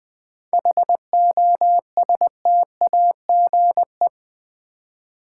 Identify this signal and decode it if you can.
Morse code, heard as a keyed tone, about 20 wpm, 700 Hz. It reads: HOSTAGE
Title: 7 Letter Words - First 3 Letters Given - 20wpm